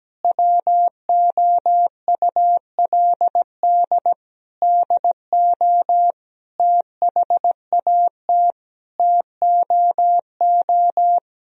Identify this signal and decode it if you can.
Morse code, heard as a keyed tone, about 17 wpm, 700 Hz. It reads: WOULD DO THAT TOO
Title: Sentences from Top 1000 Words - Review Part 3 - 17wpm